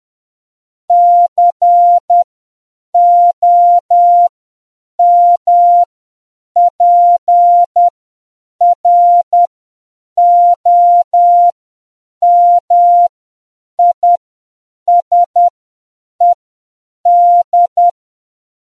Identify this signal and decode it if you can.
Morse code, heard as a keyed tone, about 10 wpm, spaced as for 7 wpm, 700 Hz. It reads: COMPROMISED